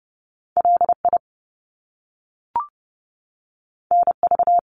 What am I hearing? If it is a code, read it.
Morse code, heard as a keyed tone, about 30 wpm, 700 Hz. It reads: LI NV
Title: Top Two Letter Combinations in Words 30wpm